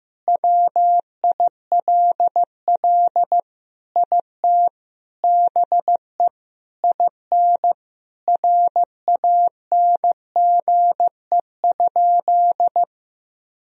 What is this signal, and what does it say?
Morse code, heard as a keyed tone, about 16 wpm, 700 Hz. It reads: WILL IT BE IN RANGE?
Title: Sentences from Top 1000 Words - Review Part 4 - 15wpm